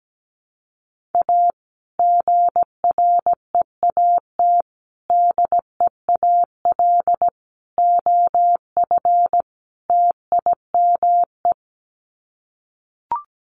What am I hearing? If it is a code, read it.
Morse code, heard as a keyed tone, about 17 wpm, 700 Hz. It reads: A GREAT DEAL OF TIME
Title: Sets of 5 Words Encore2 17wpm